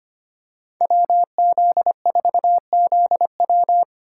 Morse code, heard as a keyed tone, about 25 wpm, 700 Hz. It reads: WZ4ZW
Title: USA Call Signs - 5 Characters - 3x Character Spacing - 25wpm